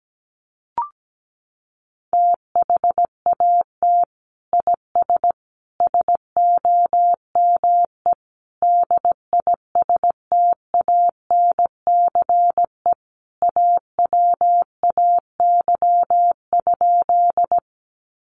Morse code, heard as a keyed tone, about 17 wpm, 700 Hz. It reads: THAT IS SOME DISTANCE AWAY?
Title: Sentences from Top 1200 Words 17wpm